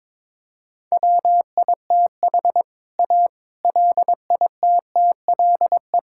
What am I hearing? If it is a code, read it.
Morse code, heard as a keyed tone, about 22 wpm, 700 Hz. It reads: WITH A LITTLE
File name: practice / 22wpm / Sets of 7 Words - 22wpm